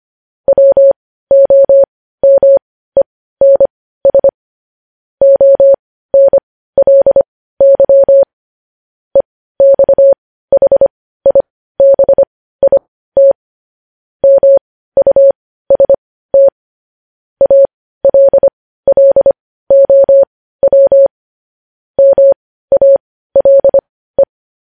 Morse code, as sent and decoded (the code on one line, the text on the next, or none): .-- --- -- . -. ... / --- -. .-.. -.-- / . -..- .... .. -... .. - / -- ..- ... - / .- .-.. .-.. --- .-- / -- .- .-.. .
WOMENS ONLY EXHIBIT MUST ALLOW MALE